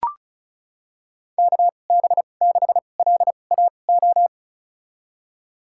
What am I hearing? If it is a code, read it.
Morse code, heard as a keyed tone, about 34 wpm, spaced as for 25 wpm, 700 Hz. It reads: KB6LAO
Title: USA Call Signs - 6 Characters - 2x Character Spacing - 35wpm